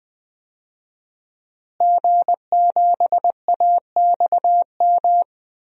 Morse code, heard as a keyed tone, about 20 wpm, 700 Hz. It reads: G7AXM
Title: DXCC-100 Call Signs - 5 Characters - Repeated 2x - 20wpm